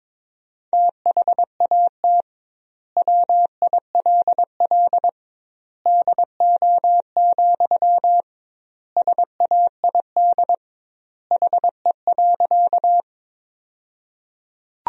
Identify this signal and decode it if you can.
Morse code, heard as a keyed tone, about 22 wpm, 700 Hz. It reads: THAT WILL DO, SAID HE.
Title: 2x Word-Spacing - Sentences from Top 100 Words - 22wpm